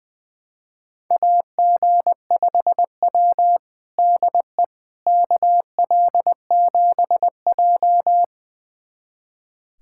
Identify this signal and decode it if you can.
Morse code, heard as a keyed tone, about 21 wpm, 700 Hz. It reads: AG5W DE KL7J
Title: USA Callsign DE Callsign - 4 Characters - 20wpm